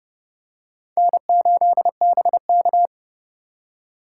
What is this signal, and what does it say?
Morse code, heard as a keyed tone, about 30 wpm, 700 Hz. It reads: N8BK
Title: USA Call Signs - 4 Characters - 3x Character Spacing - 30wpm